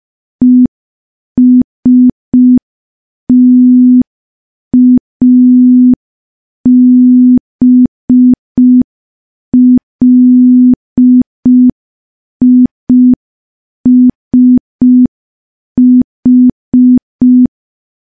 Morse code, heard as a keyed tone, about 5 wpm, 250 Hz. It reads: ESTABLISH
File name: morse / morse-training-05WPM